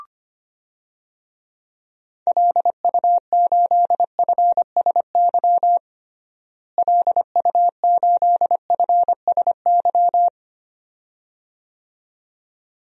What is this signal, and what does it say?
Morse code, heard as a keyed tone, about 25 wpm, 700 Hz. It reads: LU8FSY LU8FSY
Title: DXCC-100 Call Signs - 6 Characters - Repeated 2x - 25wpm